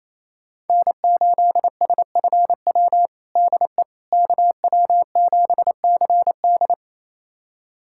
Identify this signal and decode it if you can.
Morse code, heard as a keyed tone, about 28 wpm, 700 Hz. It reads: N8SFW DE KW7CD
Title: USA Callsign DE Callsign - 5 Characters - 28wpm